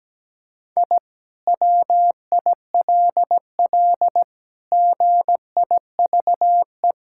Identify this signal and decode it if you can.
Morse code, heard as a keyed tone, about 17 wpm, 700 Hz. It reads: I WILL GIVE
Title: Sentences from Top 1000 Words - Review Part 6 - 17wpm